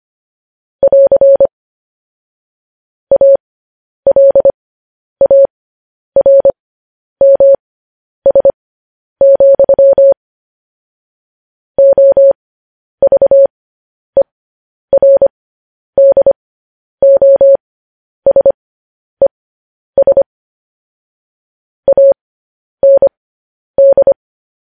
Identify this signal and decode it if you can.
Morse code, heard as a keyed tone, about 25 wpm, spaced as for 10 wpm, 550 Hz. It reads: <AR> ALARMS, OVERDOSES AND
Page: news-headlines-in-morse-code-at-10-wpm